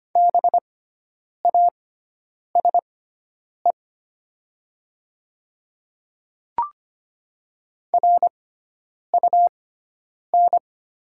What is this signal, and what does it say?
Morse code, heard as a keyed tone, about 25 wpm, spaced as for 9 wpm, 700 Hz. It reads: BASE RUN
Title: 6x Character Spacing - Top 300 Words - 25wpm